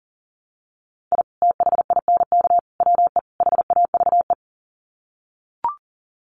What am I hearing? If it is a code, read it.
Morse code, heard as a keyed tone, about 40 wpm, 700 Hz. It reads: I THINK WE HAVE
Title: Sets of 4 Words 40wpm